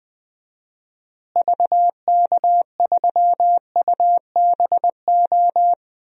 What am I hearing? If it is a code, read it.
Morse code, heard as a keyed tone, about 20 wpm, 700 Hz. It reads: VK3UBO
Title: DXCC-100 Call Signs - 6 Characters - Repeated 2x - 20wpm